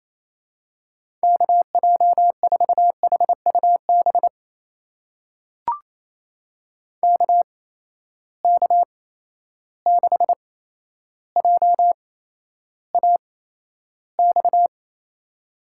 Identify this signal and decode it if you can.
Morse code, heard as a keyed tone, about 28 wpm, spaced as for 12 wpm, 700 Hz. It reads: KJ4HUB KK6JAX
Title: USA Call Signs - 6 Characters - 8x Character Spacing - 28wpm